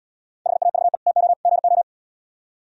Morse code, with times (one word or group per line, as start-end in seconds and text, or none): 0.5-1.0 HAVE
1.1-1.3 TO
1.5-1.8 GO